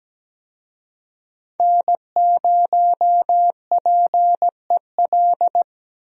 Morse code, heard as a keyed tone, about 17 wpm, 700 Hz. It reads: N0PEL
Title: USA Call Signs - 5 Characters - Repeated 3x - 17wpm